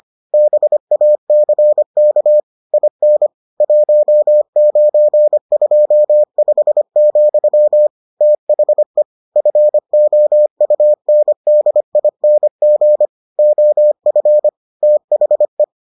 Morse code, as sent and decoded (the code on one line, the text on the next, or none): -... .- -.-. -.- / .. -. / .---- ----. ..--- ..... --..-- / - .... / ..-. --- ..- -. -.. .. -. --. / --- ..-. / - .... .
BACK IN 1925, THE FOUNDING OF THE